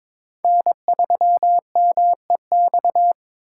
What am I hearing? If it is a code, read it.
Morse code, heard as a keyed tone, about 22 wpm, 700 Hz. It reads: N3MEX
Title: USA Call Signs - 5 Characters - 4x Character Spacing - 22wpm